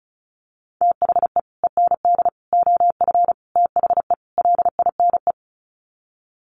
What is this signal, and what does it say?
Morse code, heard as a keyed tone, about 34 wpm, 700 Hz. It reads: THE END OF THE LINE
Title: Sets of 5 Words Encore2 35wpm